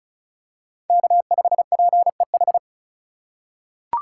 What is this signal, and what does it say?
Morse code, heard as a keyed tone, about 34 wpm, 700 Hz. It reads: K5PEH
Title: USA Call Signs - 5 Characters - 2x Character Spacing - 35wpm